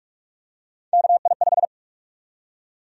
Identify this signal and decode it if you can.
Morse code, heard as a keyed tone, about 46 wpm, 700 Hz. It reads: KI5
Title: USA Call Sign Prefixes - 45wpm